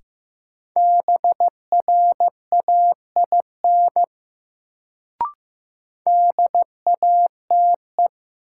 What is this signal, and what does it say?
Morse code, heard as a keyed tone, about 15 wpm, 700 Hz. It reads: BRAIN DATE
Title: Sentences from Top 1100 Words - New Word Drill Alt - 15wpm